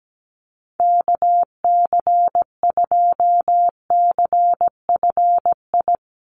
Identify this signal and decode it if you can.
Morse code, heard as a keyed tone, about 17 wpm, 700 Hz. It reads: KC2CFI
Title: USA Call Signs 17wpm